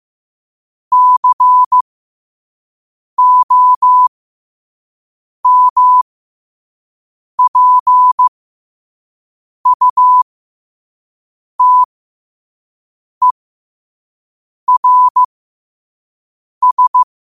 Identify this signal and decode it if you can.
Morse code, heard as a keyed tone, about 15 wpm, spaced as for 5 wpm, 1000 Hz. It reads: COMPUTERS
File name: qotdfiles / 5wpm